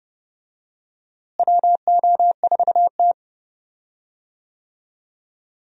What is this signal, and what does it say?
Morse code, heard as a keyed tone, about 30 wpm, 700 Hz. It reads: WO4T
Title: USA Call Signs - 4 Characters - Repeated 3x - 30wpm